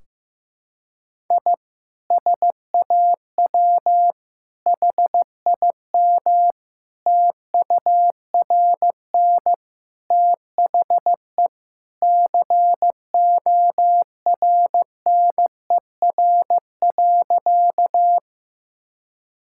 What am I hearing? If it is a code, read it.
Morse code, heard as a keyed tone, about 16 wpm, 700 Hz. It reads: I SAW HIM TURN THE CORNER.
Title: Sentences from Top 1600 Words 15wpm